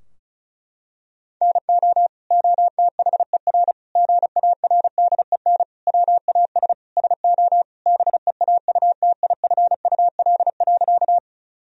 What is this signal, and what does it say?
Morse code, heard as a keyed tone, about 34 wpm, 700 Hz. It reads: NO OTHER GARDEN WAS SO BEAUTIFUL.
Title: Sentences from Top 1000 Words - Review Part 4 - 35wpm